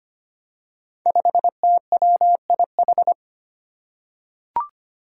Morse code, heard as a keyed tone, about 25 wpm, 700 Hz. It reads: 5TWIH
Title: Five Challenging Letter-Number Combinations 25wpm